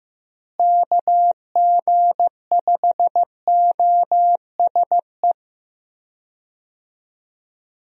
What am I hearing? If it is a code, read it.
Morse code, heard as a keyed tone, about 15 wpm, 700 Hz. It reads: KG5OSE
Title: USA Call Signs - 6 Characters - Repeated 2x - 15wpm